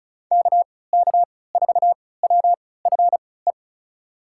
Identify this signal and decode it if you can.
Morse code, heard as a keyed tone, about 34 wpm, spaced as for 20 wpm, 700 Hz. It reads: KK4WFE
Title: USA Call Signs - 3x Character Spacing - 35wpm